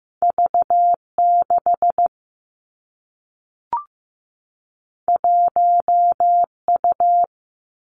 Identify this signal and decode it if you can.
Morse code, heard as a keyed tone, about 15 wpm, 700 Hz. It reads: V6 1U
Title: Two Challenging Letter-Number Combinations 15wpm